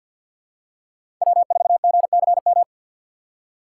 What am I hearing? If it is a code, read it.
Morse code, heard as a keyed tone, about 51 wpm, 700 Hz. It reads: W4GCM
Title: USA Call Signs - 5 Characters - 3x Character Spacing - 50wpm